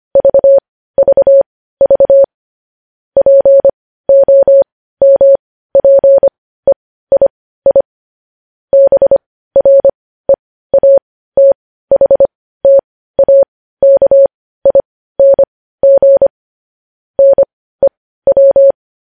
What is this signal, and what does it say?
Morse code, heard as a keyed tone, about 25 wpm, spaced as for 15 wpm, 550 Hz. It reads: VVV POMPEII BREATHTAKING NEW
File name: podcast / News headlines-1x-25x15